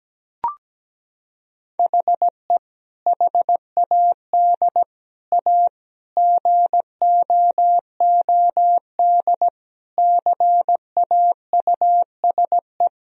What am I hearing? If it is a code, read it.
Morse code, heard as a keyed tone, about 17 wpm, 700 Hz. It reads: HE HAD A GOOD CAUSE
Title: Sentences from Top 1000 Words - Review Part 1 - 17wpm